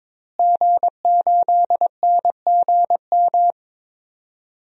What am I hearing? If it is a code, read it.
Morse code, heard as a keyed tone, about 22 wpm, 700 Hz. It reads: G8NGM
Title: DXCC-100 Call Signs - 5 Characters - Repeated 3x - 22wpm